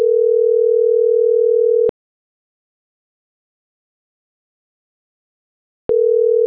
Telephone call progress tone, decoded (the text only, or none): ringback tone